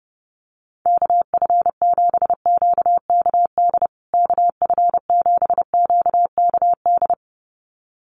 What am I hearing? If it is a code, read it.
Morse code, heard as a keyed tone, about 30 wpm, 700 Hz. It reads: KF7QKD KF7QKD
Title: USA Call Signs Repeated 2x 30wpm